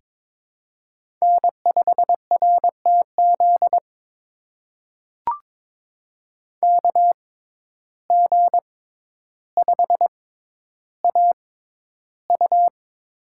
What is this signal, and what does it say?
Morse code, heard as a keyed tone, about 22 wpm, spaced as for 11 wpm, 700 Hz. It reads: N5RTZ KG5AU
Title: USA Call Signs - 5 Characters - 6x Character Spacing - 22wpm